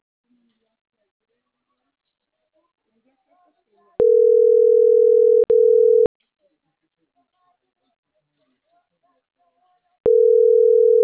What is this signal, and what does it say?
Ringback tone (North America), 440 + 480 Hz.